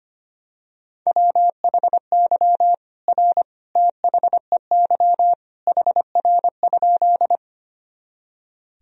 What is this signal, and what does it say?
Morse code, heard as a keyed tone, about 25 wpm, 700 Hz. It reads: WHY R THEY HR?
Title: Sentences from Top 200 Words and Abbreviations 25wpm